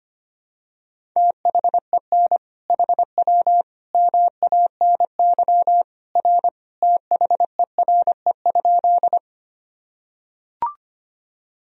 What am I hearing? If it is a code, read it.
Morse code, heard as a keyed tone, about 25 wpm, 700 Hz. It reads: THEN HW MANY R THERE?
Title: Sentences from Top 100 Words and Abbreviations 25wpm